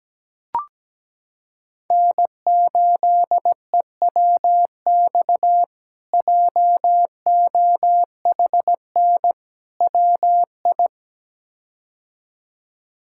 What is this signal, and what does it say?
Morse code, heard as a keyed tone, about 17 wpm, 700 Hz. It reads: N8EWX JOHN WI
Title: K1USN SST QSO Practice - 17wpm